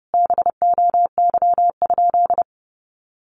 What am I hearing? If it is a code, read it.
Morse code, heard as a keyed tone, about 30 wpm, 700 Hz. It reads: BOY?
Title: Sentences from Top 1000 Words 30wpm